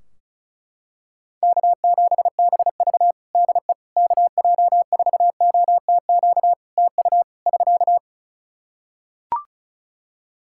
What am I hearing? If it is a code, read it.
Morse code, heard as a keyed tone, about 36 wpm, 700 Hz. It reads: K7BV DE KJ4OTQ TU <SK>
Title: Practice QSO - Exchange 4 - Ending - 35wpm